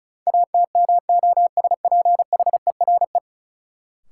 Morse code, heard as a keyed tone, about 36 wpm, 700 Hz. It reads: ATMOSPHERE
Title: Sentences from Top 1600 Words - New Word Drill Alt - 35wpm